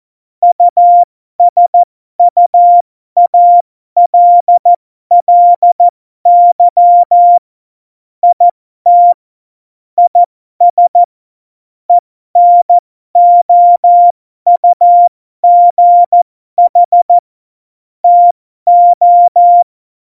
Morse code, heard as a keyed tone, about 14 wpm, spaced as for 12 wpm, 700 Hz. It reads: USUALLY IT IS ENOUGH TO
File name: Morsekurs / Witze